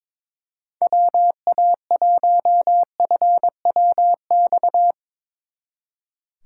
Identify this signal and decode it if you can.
Morse code, heard as a keyed tone, about 22 wpm, 700 Hz. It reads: WA1FWX
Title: Speed Racing - USA 2x3 Call Signs 22wpm